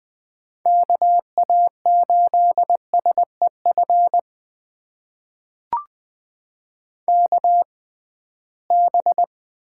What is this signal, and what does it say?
Morse code, heard as a keyed tone, about 21 wpm, spaced as for 15 wpm, 700 Hz. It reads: KA8SEF KB
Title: USA Call Signs - 6 Characters - 6x Character Spacing - 20wpm